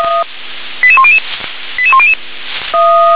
Five-tone selective call (ZVEI), heard as quick runs of five tones, tones of about 70 ms. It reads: 88100 88100
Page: SDR/ZVEI